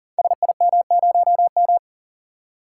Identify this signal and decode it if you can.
Morse code, heard as a keyed tone, about 40 wpm, 690 Hz. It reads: SIM0M
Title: Code Groups - 5 Letter:Numbers - 40wpm